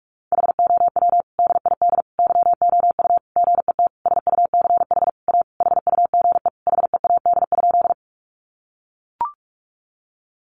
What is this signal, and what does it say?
Morse code, heard as a keyed tone, about 44 wpm, 700 Hz. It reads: HOW DID YOU GET SUCH A HUGE HEAD?
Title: Sentences from Top 1100 Words 45wpm